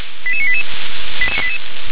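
Five-tone selective call (ZVEI), heard as quick runs of five tones, tones of about 70 ms. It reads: 88988
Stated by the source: ZVEI five-tone call 88953 received by SDR.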